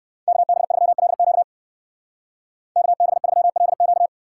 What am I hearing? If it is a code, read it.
Morse code, heard as a keyed tone, about 59 wpm, 700 Hz. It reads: KB3LQ KB3LQ